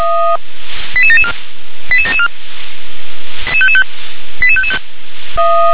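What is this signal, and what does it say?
Five-tone selective call (ZVEI), heard as quick runs of five tones, tones of about 70 ms. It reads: 88774 88774 88555
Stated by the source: ZVEI five-tone call 88865 received by SDR.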